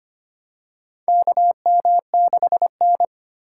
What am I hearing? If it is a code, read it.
Morse code, heard as a keyed tone, about 25 wpm, 700 Hz. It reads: KM6N
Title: USA Call Signs - 4 Characters - 4x Character Spacing - 25wpm